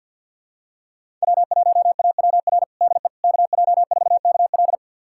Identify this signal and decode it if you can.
Morse code, heard as a keyed tone, about 52 wpm, 700 Hz. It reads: W1AWR DE KJ4KL
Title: USA Callsign DE Callsign - 5 Characters - 50wpm